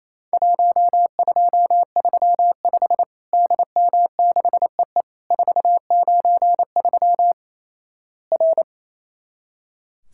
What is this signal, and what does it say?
Morse code, heard as a keyed tone, about 28 wpm, 700 Hz. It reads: 1235 DM6EE 493 R
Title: Worked All Europe DX Contest - QTC Exchange - 28wpm